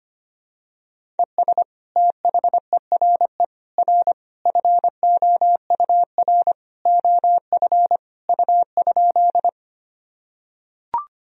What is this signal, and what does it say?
Morse code, heard as a keyed tone, about 25 wpm, 700 Hz. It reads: ES THERE R FOUR OF U?
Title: Sentences from Top 300 Words and Abbreviations 25wpm